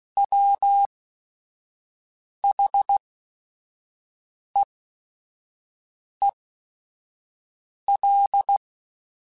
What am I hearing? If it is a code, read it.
Morse code, heard as a keyed tone, about 16 wpm, spaced as for 5 wpm, 800 Hz. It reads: WHEEL